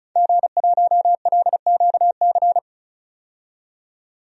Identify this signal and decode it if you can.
Morse code, heard as a keyed tone, about 36 wpm, 700 Hz. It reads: G1LQC
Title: DXCC-100 Call Signs - 5 Characters - 35wpm